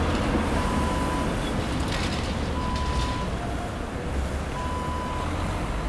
There is no code.